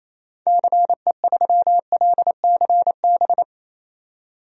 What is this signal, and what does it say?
Morse code, heard as a keyed tone, about 28 wpm, 700 Hz. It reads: CE3LCB
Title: DXCC-100 Call Signs - 6 Characters - 28wpm